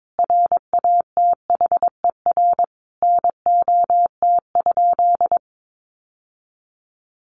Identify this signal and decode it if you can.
Morse code, heard as a keyed tone, about 22 wpm, 700 Hz. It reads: RATHER NOT?
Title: Sentences from Top 700 Words 22wpm